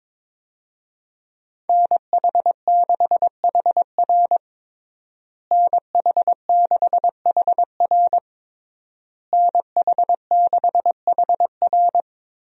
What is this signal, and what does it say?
Morse code, heard as a keyed tone, about 22 wpm, 700 Hz. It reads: NH6HR NH6HR NH6HR
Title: USA Call Signs - 5 Characters - Repeated 3x - 22wpm